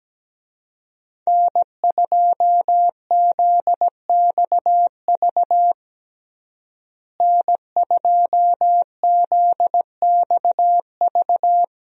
Morse code, heard as a keyed tone, about 17 wpm, 700 Hz. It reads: N2ZXV N2ZXV
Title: USA Call Signs - 5 Characters - Repeated 3x - 17wpm